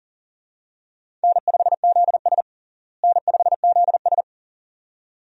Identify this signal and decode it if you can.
Morse code, heard as a keyed tone, about 40 wpm, 700 Hz. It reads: N5ZS N5ZS
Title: USA Call Signs - 4 Characters - Repeated 2x - 40wpm